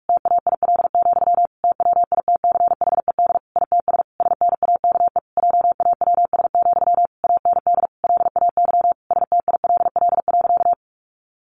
Morse code, heard as a keyed tone, about 44 wpm, 700 Hz. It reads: TAIL, TWITCHED ITS SNAKE JAWS, AND LAY STILL.
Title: A Sound Of Thunder 45wpm